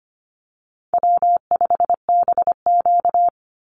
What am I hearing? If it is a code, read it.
Morse code, heard as a keyed tone, about 25 wpm, 700 Hz. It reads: W5BQ
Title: Easy Call Signs - 25wpm